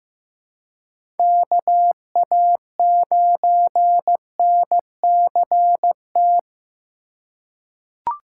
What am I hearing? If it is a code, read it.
Morse code, heard as a keyed tone, about 15 wpm, 700 Hz. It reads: KA9NCT
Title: USA Call Signs - 2x Character Spacing - 15wpm